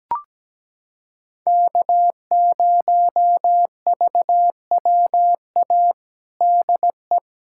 Morse code, heard as a keyed tone, about 17 wpm, 700 Hz. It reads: K0VWA DE
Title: USA Callsign DE Callsign 17wpm